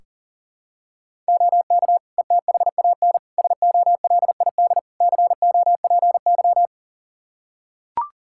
Morse code, heard as a keyed tone, about 40 wpm, 690 Hz. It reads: OK ETHAN SOLID COPY
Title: Practice QSO - Exchange 3 - More Info - 40wpm